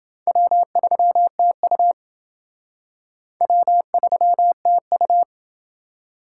Morse code, heard as a keyed tone, about 28 wpm, 700 Hz. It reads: W3TU W3TU